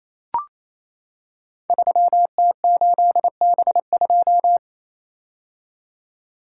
Morse code, heard as a keyed tone, about 28 wpm, 700 Hz. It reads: 3T8B2
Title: Code Groups - 5 Letter:Numbers - 28wpm